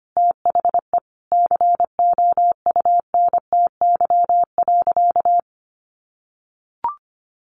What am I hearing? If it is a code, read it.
Morse code, heard as a keyed tone, about 25 wpm, 700 Hz. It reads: THE COUNTY.
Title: Sentences from Top 1500 Words 25wpm